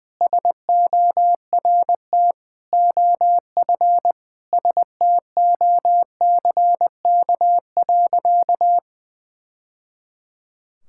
Morse code, heard as a keyed tone, about 21 wpm, 700 Hz. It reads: SORT OF STOCK.
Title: Sentences from Top 1000 Words - Review Part 5 - 20wpm